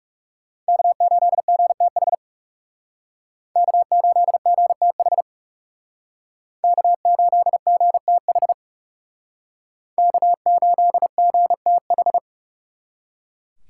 Morse code, heard as a keyed tone, about 36 wpm, 700 Hz. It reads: K8GTH K8GTH K8GTH K8GTH
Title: Speed Racing - USA 1x3 Call Signs 30wpm